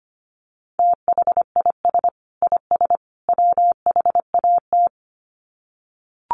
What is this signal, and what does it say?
Morse code, heard as a keyed tone, about 25 wpm, 700 Hz. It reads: THIS IS WHAT
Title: Sets of 3 Words 25wpm